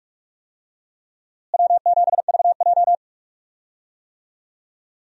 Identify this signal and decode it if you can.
Morse code, heard as a keyed tone, about 43 wpm, 700 Hz. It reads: W7VJ
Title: USA Call Signs - 4 Characters - Repeated 2x - 45wpm